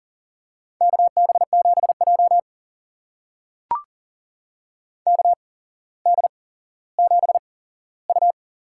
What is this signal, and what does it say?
Morse code, heard as a keyed tone, about 40 wpm, spaced as for 17 wpm, 700 Hz. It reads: KB7J KD7U
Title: USA Call Signs - 8x Character Spacing - 40wpm